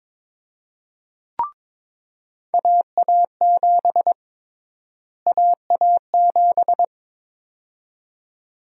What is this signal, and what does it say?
Morse code, heard as a keyed tone, about 21 wpm, 700 Hz. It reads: AA7 AA7
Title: USA Call Sign Prefixes - Repeated 2x - 22wpm